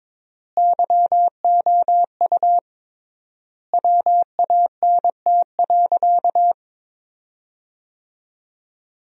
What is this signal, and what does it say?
Morse code, heard as a keyed tone, about 22 wpm, 700 Hz. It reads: YOU WANT.
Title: 3x Word-Spacing - Sentences from Top 200 Words - 22wpm